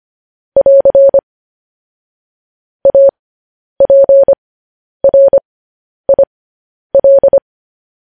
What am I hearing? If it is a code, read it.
Morse code, heard as a keyed tone, about 25 wpm, spaced as for 10 wpm, 550 Hz. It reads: <AR> APRIL
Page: News Headlines in Morse Code at 10 WPM